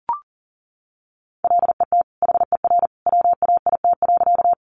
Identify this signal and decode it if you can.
Morse code, heard as a keyed tone, about 40 wpm, 700 Hz. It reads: LET HER WAIT.
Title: Sentences from Top 500 Words 40wpm